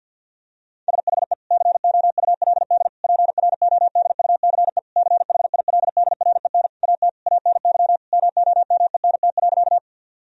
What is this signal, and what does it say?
Morse code, heard as a keyed tone, about 50 wpm, 700 Hz. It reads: SHE COULD PRODUCE CHILDREN AT ANY MOMENT.
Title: Sentences from Top 1000 Words - Review Part 3 - 50wpm